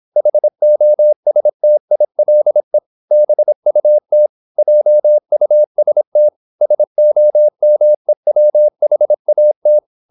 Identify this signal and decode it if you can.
Morse code, heard as a keyed tone, about 26 wpm, 600 Hz. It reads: HOSTILE BUT JUST SOMEWHAT
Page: Never 26wpm | CW med Gnister